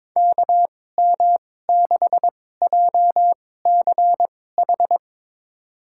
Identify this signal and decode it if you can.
Morse code, heard as a keyed tone, about 22 wpm, spaced as for 16 wpm, 700 Hz. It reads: KM6JCH
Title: USA Call Signs - 2x Character Spacing - 22wpm